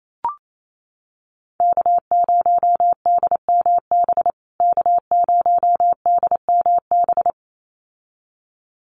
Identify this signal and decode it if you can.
Morse code, heard as a keyed tone, about 28 wpm, 700 Hz. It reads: K0DMB K0DMB
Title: USA Call Signs Repeated 2x 28wpm